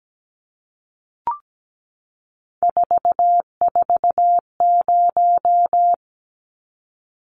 Morse code, heard as a keyed tone, about 17 wpm, 700 Hz. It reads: 440
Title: Numbers - 3-Digit 17wpm